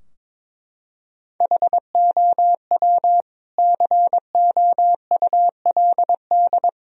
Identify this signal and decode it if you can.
Morse code, heard as a keyed tone, about 22 wpm, 700 Hz. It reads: HOW COULD
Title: Sentences from Top 1000 Words - Review Part 2 - 22wpm